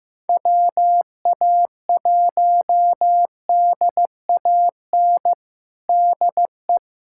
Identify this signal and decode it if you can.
Morse code, heard as a keyed tone, about 15 wpm, 700 Hz. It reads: WA1DAN DE
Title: USA Callsign DE Callsign 15wpm